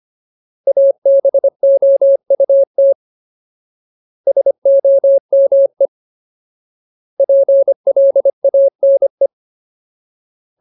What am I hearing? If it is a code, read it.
Morse code, heard as a keyed tone, about 25 wpm, 550 Hz. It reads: ABOUT SOME PLANE
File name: Ultimate Word List / 25WPM Ultimate